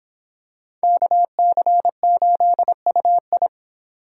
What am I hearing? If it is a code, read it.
Morse code, heard as a keyed tone, about 26 wpm, 700 Hz. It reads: KC8UI